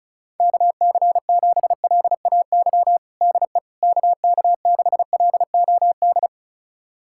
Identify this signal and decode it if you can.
Morse code, heard as a keyed tone, about 34 wpm, 690 Hz. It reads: KC7LAY DE KK6LOD
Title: USA Callsign DE Callsign - 6 Characters - 35wpm